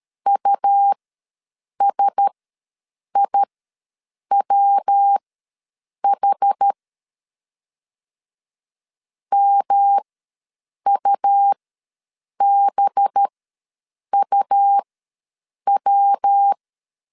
Morse code, heard as a keyed tone, about 13 wpm, spaced as for 7 wpm, 800 Hz. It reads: USIWH MUBUW
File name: LEZIONI / 5 LBVC / 5 LB ws 12wpm 3034